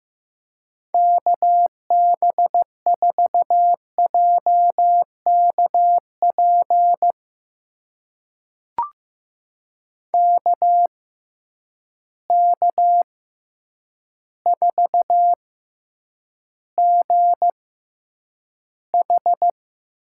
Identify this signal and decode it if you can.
Morse code, heard as a keyed tone, about 15 wpm, spaced as for 8 wpm, 700 Hz. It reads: KB4JKP KK4GH
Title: USA Call Signs - 6 Characters - 6x Character Spacing - 15wpm